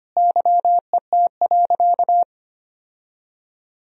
Morse code, heard as a keyed tone, about 25 wpm, 700 Hz. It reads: YET.
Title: Sentences from Top 1000 Words - Review Part 2 - 25wpm